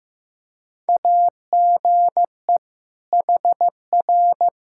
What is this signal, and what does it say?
Morse code, heard as a keyed tone, about 15 wpm, 700 Hz. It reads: AGE HR
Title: K1USN SST QSO Practice 15wpm